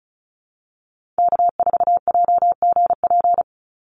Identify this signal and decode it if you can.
Morse code, heard as a keyed tone, about 34 wpm, 700 Hz. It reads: K4JGP
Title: USA Call Signs 35wpm